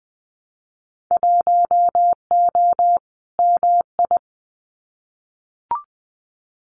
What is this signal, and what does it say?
Morse code, heard as a keyed tone, about 20 wpm, 700 Hz. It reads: 1O MI
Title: Winter Field Day Exchanges Encore 20wpm